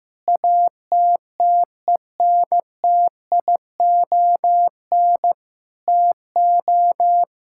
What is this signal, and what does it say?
Morse code, heard as a keyed tone, about 15 wpm, 700 Hz. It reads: ATTENTION TO
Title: Sentences from Top 1000 Words - Review Part 6 - 15wpm